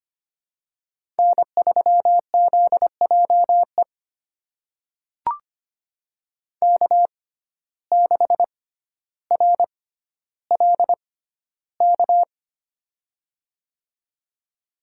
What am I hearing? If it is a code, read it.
Morse code, heard as a keyed tone, about 25 wpm, spaced as for 13 wpm, 700 Hz. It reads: N3ZJE K6RLK
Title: USA Call Signs - 5 Characters - 6x Character Spacing - 25wpm